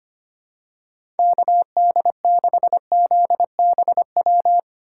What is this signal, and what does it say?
Morse code, heard as a keyed tone, about 25 wpm, 700 Hz. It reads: KD6ZBW
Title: USA Call Signs - 6 Characters - 3x Character Spacing - 25wpm